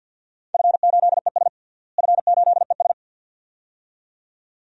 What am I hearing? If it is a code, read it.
Morse code, heard as a keyed tone, about 47 wpm, 700 Hz. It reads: F8ES F8ES
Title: DXCC-100 Call Signs - 4 Characters - Repeated 2x - 50wpm